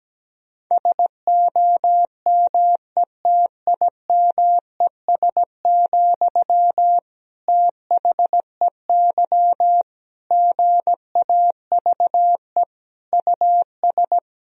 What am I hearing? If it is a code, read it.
Morse code, heard as a keyed tone, about 17 wpm, 700 Hz. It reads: SOMETIMES, THEY GAVE US
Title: Sentences from Top 1000 Words - Review Part 4 - 17wpm